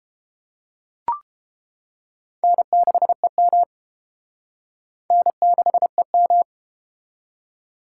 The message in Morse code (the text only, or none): N6EM N6EM